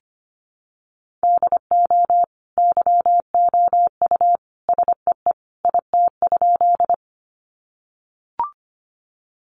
Morse code, heard as a keyed tone, about 25 wpm, 700 Hz. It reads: DO YOU SEE IT?
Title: Sentences from Top 100 Words 25wpm